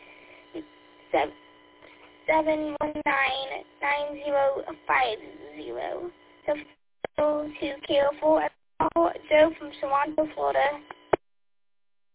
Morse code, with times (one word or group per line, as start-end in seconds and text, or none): none